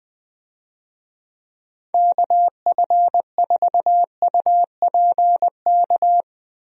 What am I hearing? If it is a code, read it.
Morse code, heard as a keyed tone, about 20 wpm, 700 Hz. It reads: KF4UPK
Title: USA Call Signs - 6 Characters - Repeated 3x - 20wpm